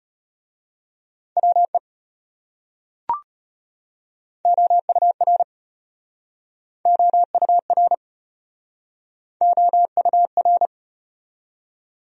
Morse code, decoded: WE OUR OUR OUR

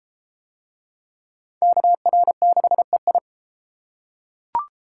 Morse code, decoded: KR6EI

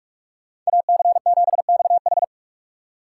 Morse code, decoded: AC7XH

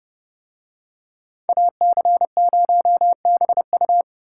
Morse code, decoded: AC0BU

